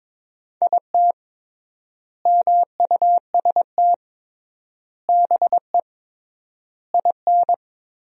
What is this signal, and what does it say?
Morse code, heard as a keyed tone, about 22 wpm, 700 Hz. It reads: IT MUST BE IN
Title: 3x Word-Spacing - Sentences from Top 300 Words - 22wpm